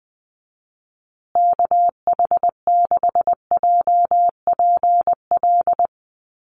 Morse code, heard as a keyed tone, about 20 wpm, 700 Hz. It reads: KH6JPL
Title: USA Call Signs Repeated 3x 20wpm